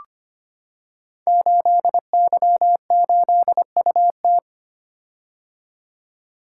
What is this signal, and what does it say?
Morse code, heard as a keyed tone, about 25 wpm, 700 Hz. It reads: 8Y8UT
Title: Code Groups - 5 Letter:Numbers - 25wpm